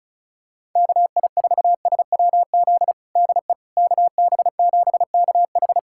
Morse code, heard as a keyed tone, about 36 wpm, 700 Hz. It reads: KI4SWZ DE KB7KH
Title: USA Callsign DE Callsign 35wpm